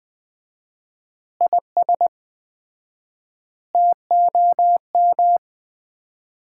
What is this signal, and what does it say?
Morse code, heard as a keyed tone, about 20 wpm, 700 Hz. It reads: IS TOM
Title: Common QSO Phrases - 4x Word-Spacing - 20wpm